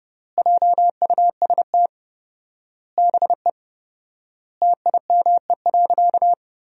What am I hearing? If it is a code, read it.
Morse code, heard as a keyed tone, about 30 wpm, 700 Hz. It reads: JUST BE TIME.
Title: 4x Word-Spacing - Sentences from Top 200 Words - 30wpm